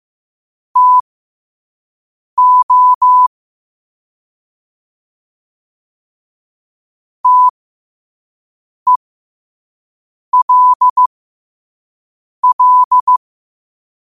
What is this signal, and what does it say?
Morse code, heard as a keyed tone, about 15 wpm, spaced as for 5 wpm, 1000 Hz. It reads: TO TELL
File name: qotdfiles / 5wpm